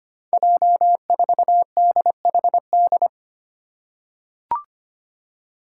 Morse code, heard as a keyed tone, about 25 wpm, 700 Hz. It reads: J4DHD
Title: Five Challenging Letter-Number Combinations 25wpm